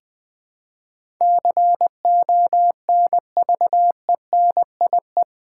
Morse code, heard as a keyed tone, about 20 wpm, 700 Hz. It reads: CONVENIE
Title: Build A Word - 4 Letters At A Time - 11-Letter Words - 20wpm